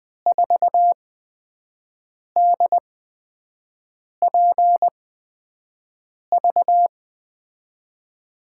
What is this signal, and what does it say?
Morse code, heard as a keyed tone, about 20 wpm, spaced as for 5 wpm, 700 Hz. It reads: 4DPV
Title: USA Call Signs - 8x Character Spacing - 20wpm